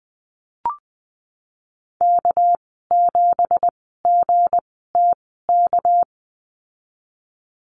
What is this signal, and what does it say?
Morse code, heard as a keyed tone, about 20 wpm, spaced as for 15 wpm, 700 Hz. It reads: K7GTK
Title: 207 Lesson 33g - Letter j - Call Signs 20wpm